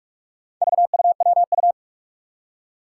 Morse code, heard as a keyed tone, about 46 wpm, 700 Hz. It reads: VUWU
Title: Four Challenging Letter-Number Combinations 45wpm